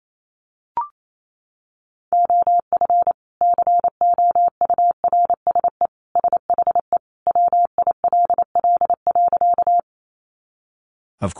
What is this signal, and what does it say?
Morse code, heard as a keyed tone, about 28 wpm, 700 Hz. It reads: OF COURSE SHE WILL.
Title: Sentences from Top 500 Words 28wpm